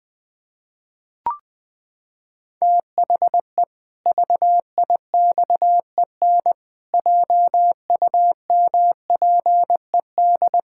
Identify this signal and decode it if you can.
Morse code, heard as a keyed tone, about 21 wpm, 700 Hz. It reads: THE VIXEN JUMPED
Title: Pangram Challenge v2 - 20wpm